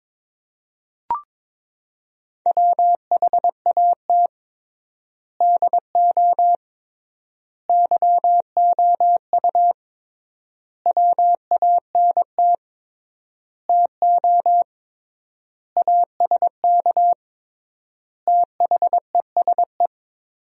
Morse code, heard as a keyed tone, about 21 wpm, 700 Hz. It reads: WHAT DO YOU WANT TO ASK THESE
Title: 3x Word-Spacing - Sentences from Top 200 Words - 22wpm